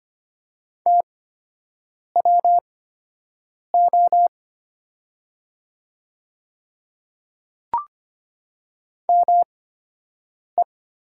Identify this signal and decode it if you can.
Morse code, heard as a keyed tone, about 25 wpm, spaced as for 7 wpm, 700 Hz. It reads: TWO ME
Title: 8x Character Spacing - Top 300 Words - 25wpm